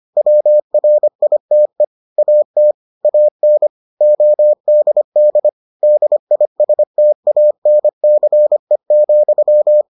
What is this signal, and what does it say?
Morse code, heard as a keyed tone, about 25 wpm, 600 Hz. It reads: WRITE AT AN ODD DISTANCE,